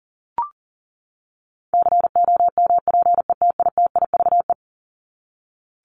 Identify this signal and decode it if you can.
Morse code, heard as a keyed tone, about 40 wpm, 700 Hz. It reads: COMPETITIVE
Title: Top 1001-1250 Words 40wpm